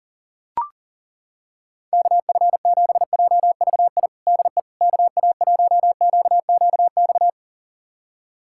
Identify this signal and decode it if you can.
Morse code, heard as a keyed tone, about 40 wpm, 700 Hz. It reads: KF7JVI DE KA1QQX
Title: USA Callsign DE Callsign - 6 Characters - 40wpm